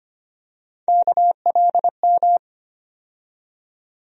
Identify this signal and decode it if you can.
Morse code, heard as a keyed tone, about 25 wpm, 700 Hz. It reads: KLM
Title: USA Call Sign Postfixes - 3 Letters 25wpm